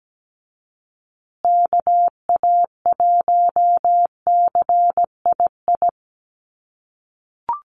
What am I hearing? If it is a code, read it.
Morse code, heard as a keyed tone, about 17 wpm, 700 Hz. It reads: KA1CII
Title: USA Call Signs Repeated 3x 17wpm